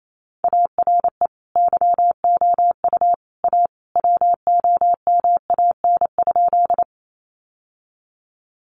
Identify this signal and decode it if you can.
Morse code, heard as a keyed tone, about 28 wpm, 700 Hz. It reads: ARE YOU A WOMAN?
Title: Sentences from Top 1100 Words 28wpm